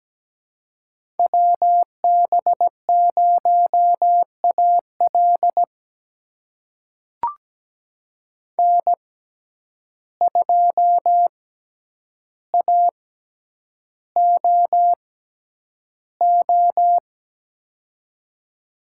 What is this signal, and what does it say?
Morse code, heard as a keyed tone, about 17 wpm, spaced as for 9 wpm, 700 Hz. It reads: WB0AL N2AOO
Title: USA Call Signs - 5 Characters - 6x Character Spacing - 17wpm